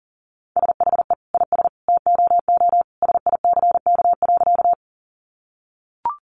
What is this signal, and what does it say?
Morse code, heard as a keyed tone, about 40 wpm, 700 Hz. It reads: SHE IS TOO SICK.
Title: Easy Phrases 40wpm